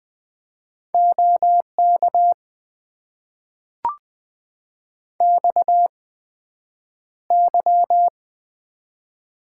Morse code, heard as a keyed tone, about 20 wpm, spaced as for 9 wpm, 700 Hz. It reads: OK XY
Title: Top 90 QSO Elements - 8x Character Spacing - 20wpm